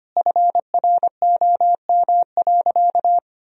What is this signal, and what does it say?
Morse code, heard as a keyed tone, about 25 wpm, 700 Hz. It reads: FROM.
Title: Sentences from Top 1000 Words - Review Part 4 - 25wpm